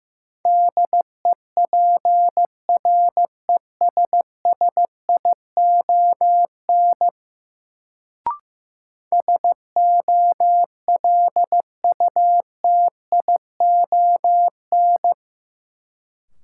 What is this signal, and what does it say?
Morse code, heard as a keyed tone, about 15 wpm, 700 Hz. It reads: DEPRESSION SOLUTION
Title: Sentences from Top 1300 Words - New Word Drill Alt - 15wpm